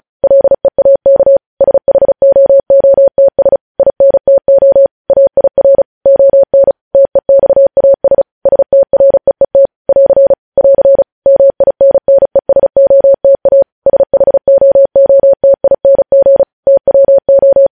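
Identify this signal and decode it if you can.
Morse code, heard as a keyed tone, about 36 wpm, 550 Hz. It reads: LEAK SHOOTS INTO AIR ON TEXAS STREET <AR> <AR> MINNESOTA SHOOTING TWO
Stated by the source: Morse code news headlines sent at 35 wpm.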